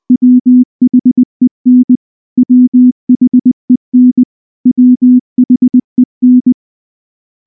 Morse code, heard as a keyed tone, about 20 wpm, 250 Hz. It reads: WHEN WHEN WHEN
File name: morse-training-20WPM